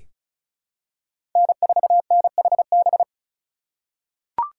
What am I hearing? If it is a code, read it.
Morse code, heard as a keyed tone, about 34 wpm, 700 Hz. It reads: N4NHB
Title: USA Call Signs - 5 Characters - 4x Character Spacing - 35wpm